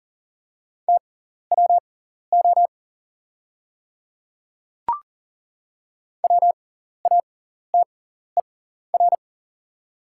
Morse code, heard as a keyed tone, about 40 wpm, spaced as for 14 wpm, 690 Hz. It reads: TWO WATER
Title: 6x Character Spacing - Top 200 Words - 40wpm